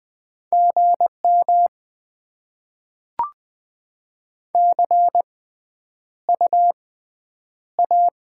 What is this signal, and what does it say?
Morse code, heard as a keyed tone, about 20 wpm, spaced as for 10 wpm, 700 Hz. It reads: GM CU A
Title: Top 20 QSO Elements - 6x Character Spacing - 20wpm